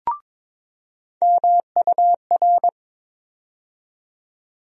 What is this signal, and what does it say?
Morse code, heard as a keyed tone, about 22 wpm, 700 Hz. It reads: MUR
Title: USA Call Sign Postfixes - 3 Letters 22wpm